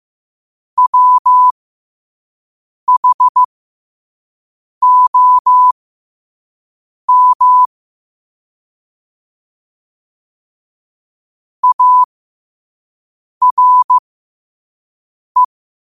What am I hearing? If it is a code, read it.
Morse code, heard as a keyed tone, about 15 wpm, spaced as for 5 wpm, 1000 Hz. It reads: WHOM ARE
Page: Quotes for Sat, 16 Aug 2025 in Morse Code at 5 words per minute.